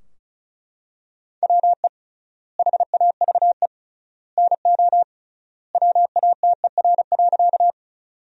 Morse code, heard as a keyed tone, about 36 wpm, 700 Hz. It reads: WE HAVE NO WATER.
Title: 3x Word-Spacing - Sentences from Top 100 Words - 35wpm